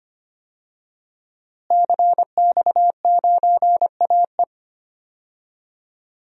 Morse code, heard as a keyed tone, about 25 wpm, 700 Hz. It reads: CX9AE
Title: DXCC-100 Call Signs - 5 Characters - Repeated 2x - 25wpm